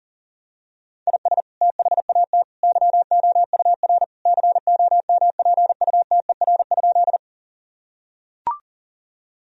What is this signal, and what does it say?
Morse code, heard as a keyed tone, about 40 wpm, 700 Hz. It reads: IS THAT YOUR COMPUTER?
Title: Sentences from Top 1000 Words - Review Part 3 - 40wpm